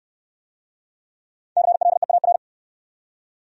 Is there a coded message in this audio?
Yes: Morse code from 1.6 to 2.4 s, heard as a keyed tone, about 64 wpm, 700 Hz. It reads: CLEAR